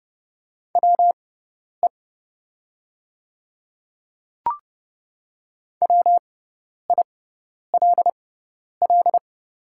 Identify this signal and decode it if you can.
Morse code, heard as a keyed tone, about 30 wpm, spaced as for 10 wpm, 700 Hz. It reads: WE WILL
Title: 6x Character Spacing - Top 100 Words - 30wpm